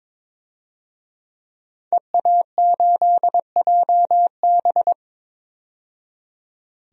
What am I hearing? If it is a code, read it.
Morse code, heard as a keyed tone, about 22 wpm, 700 Hz. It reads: EA8JB
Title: DXCC-100 Call Signs - 5 Characters - 22wpm